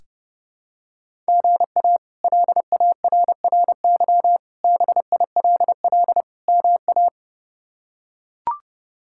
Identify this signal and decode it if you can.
Morse code, heard as a keyed tone, about 30 wpm, 700 Hz. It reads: GA LARRY BILL MA
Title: K1USN SST QSO Practice - 30wpm